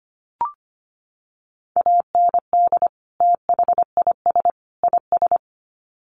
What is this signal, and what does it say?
Morse code, heard as a keyed tone, about 25 wpm, 700 Hz. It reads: AND THIS IS
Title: Sets of 3 Words 25wpm